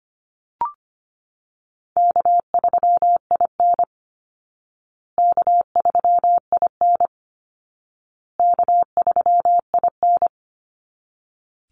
Morse code, heard as a keyed tone, about 25 wpm, 700 Hz. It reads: K3IN K3IN K3IN
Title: Easy Call Signs Repeated 3x - 25wpm